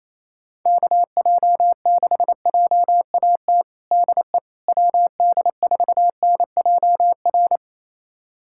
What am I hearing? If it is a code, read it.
Morse code, heard as a keyed tone, about 28 wpm, 700 Hz. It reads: KJ6JAT DE WD4NJR
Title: USA Callsign DE Callsign - 6 Characters - 28wpm